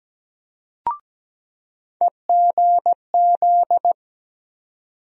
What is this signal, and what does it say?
Morse code, heard as a keyed tone, about 17 wpm, 700 Hz. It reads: EGZ
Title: USA Call Sign Postfixes - 3 Letters 17wpm